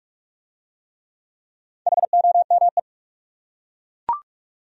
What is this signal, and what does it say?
Morse code, heard as a keyed tone, about 42 wpm, 700 Hz. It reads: HOME